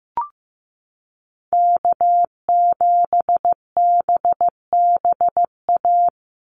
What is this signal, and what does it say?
Morse code, heard as a keyed tone, about 15 wpm, 700 Hz. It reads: K7BBA
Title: USA Call Signs 15wpm